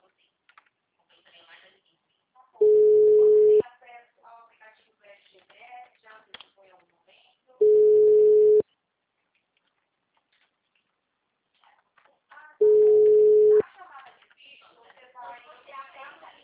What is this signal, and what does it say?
Ringback tone (Europe, 425 Hz), with speech.
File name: out- -2004- - - .68